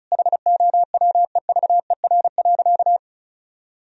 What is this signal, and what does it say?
Morse code, heard as a keyed tone, about 36 wpm, 700 Hz. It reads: HOWEVER.
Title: Sentences from Top 1000 Words - Review Part 4 - 35wpm